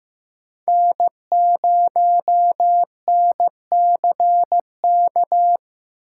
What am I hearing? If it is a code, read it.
Morse code, heard as a keyed tone, about 15 wpm, 700 Hz. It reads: N0NCK
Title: USA Call Signs - 5 Characters - 2x Character Spacing - 15wpm